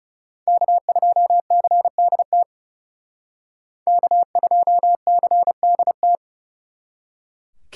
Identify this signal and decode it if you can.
Morse code, heard as a keyed tone, about 36 wpm, 700 Hz. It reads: K2CDT K2CDT
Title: Speed Racing - USA 1x3 Call Signs 30wpm